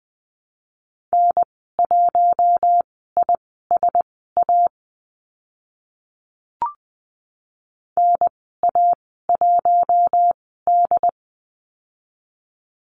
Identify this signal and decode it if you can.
Morse code, heard as a keyed tone, about 20 wpm, spaced as for 14 wpm, 700 Hz. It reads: N1ISA NA1D
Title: 088 Lesson 15g - Number 5 - Call Signs 20wpm